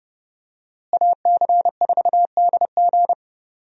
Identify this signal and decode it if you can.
Morse code, heard as a keyed tone, about 30 wpm, 700 Hz. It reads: AC4DG
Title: USA Call Signs - 5 Characters - Repeated 3x - 30wpm